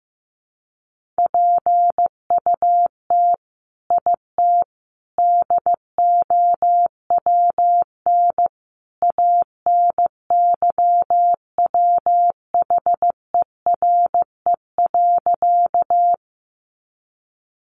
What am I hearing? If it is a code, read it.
Morse code, heard as a keyed tone, about 15 wpm, 700 Hz. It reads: PUT IT DOWN ANYWHERE.
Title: Sentences from Top 1400 Words 15wpm